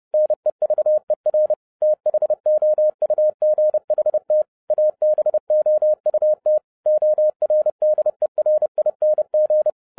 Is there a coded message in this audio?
Yes: Morse code from 0.1 to 9.7 s, heard as a keyed tone, about 30 wpm, 610 Hz. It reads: NEVER THOUGHT ABOUT ORDERING